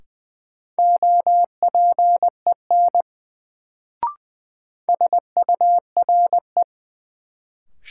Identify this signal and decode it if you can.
Morse code, heard as a keyed tone, about 20 wpm, 700 Hz. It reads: OPEN SURE
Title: Sentences from Top 300 Words - New Word Drill - 20wpm